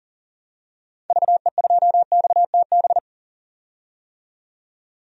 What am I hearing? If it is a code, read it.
Morse code, heard as a keyed tone, about 40 wpm, 700 Hz. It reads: VE2XTB
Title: DXCC-100 Call Signs - 6 Characters - 40wpm